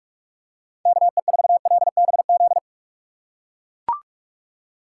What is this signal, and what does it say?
Morse code, heard as a keyed tone, about 43 wpm, 700 Hz. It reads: KE4LBZ
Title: USA Call Signs - 6 Characters - 2x Character Spacing - 45wpm